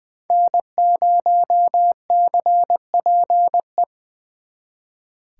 Morse code, heard as a keyed tone, about 23 wpm, 700 Hz. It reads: N0CPE
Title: Speed Racing - USA 1x3 Call Signs 20wpm